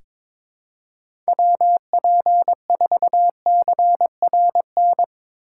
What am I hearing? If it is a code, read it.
Morse code, heard as a keyed tone, about 22 wpm, 700 Hz. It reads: WP4CRN
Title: USA Call Signs - 6 Characters - 4x Character Spacing - 22wpm